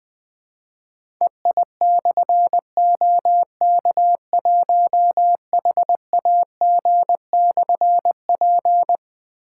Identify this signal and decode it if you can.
Morse code, heard as a keyed tone, about 20 wpm, 700 Hz. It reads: EI/OK1HAG/P
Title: Difficult Call Signs 20wpm